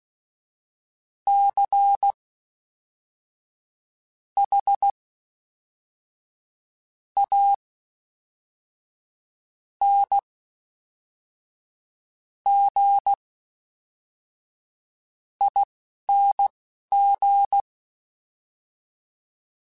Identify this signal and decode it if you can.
Morse code, heard as a keyed tone, about 16 wpm, spaced as for 4 wpm, 800 Hz. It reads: CHANGING
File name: Special INGs / Special INGs-mp3s / Fast ing @10